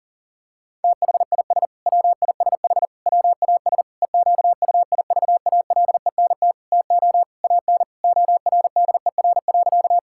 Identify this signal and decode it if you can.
Morse code, heard as a keyed tone, about 40 wpm, 700 Hz. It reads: THIS WISH WAS EQUIVALENT TO AN ORDER.
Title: Sentences from Top 1600 Words 40wpm